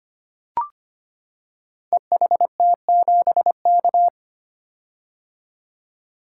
Morse code, heard as a keyed tone, about 25 wpm, 700 Hz. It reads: EHT7K
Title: Code Groups - 5 Letter:Numbers - 25wpm